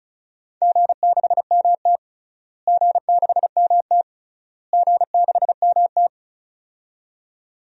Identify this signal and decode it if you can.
Morse code, heard as a keyed tone, about 36 wpm, 700 Hz. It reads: G6MT G6MT G6MT
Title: DXCC-100 Call Signs - 4 Characters - Repeated 3x - 35wpm